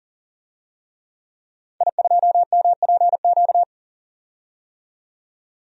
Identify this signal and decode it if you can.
Morse code, heard as a keyed tone, about 40 wpm, 700 Hz. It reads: I2MPQ
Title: DXCC-100 Call Signs - 5 Characters - 40wpm